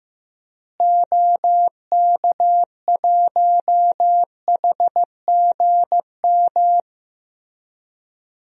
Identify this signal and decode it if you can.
Morse code, heard as a keyed tone, about 15 wpm, 700 Hz. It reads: OK1HGM
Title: DXCC-100 Call Signs - 6 Characters - Repeated 2x - 15wpm